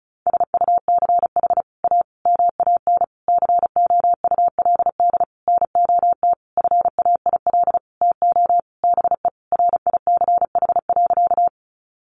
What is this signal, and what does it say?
Morse code, heard as a keyed tone, about 34 wpm, 700 Hz. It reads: SUCH A MAN COULD NOT FAIL TO BE RICH.
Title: Sentences from Top 1100 Words 35wpm